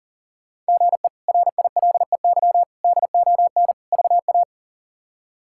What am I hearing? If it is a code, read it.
Morse code, heard as a keyed tone, about 40 wpm, 700 Hz. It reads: GE RILEY DON VA
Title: K1USN SST QSO Practice - 40wpm